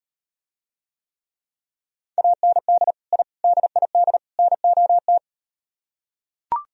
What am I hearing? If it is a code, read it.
Morse code, heard as a keyed tone, about 38 wpm, 700 Hz. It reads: AND I DID NOT